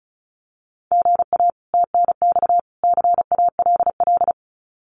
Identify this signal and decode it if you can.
Morse code, heard as a keyed tone, about 34 wpm, 700 Hz. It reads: GA TNX CALL
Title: Common QSO Phrases 35wpm